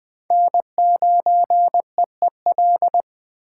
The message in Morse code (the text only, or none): N9EEL